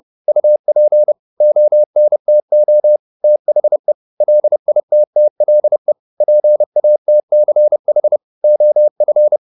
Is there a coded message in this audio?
Yes: Morse code from 0.3 to 9.4 s, heard as a keyed tone, about 30 wpm, 600 Hz. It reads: UP ONTO THE LITTLE PATCH OF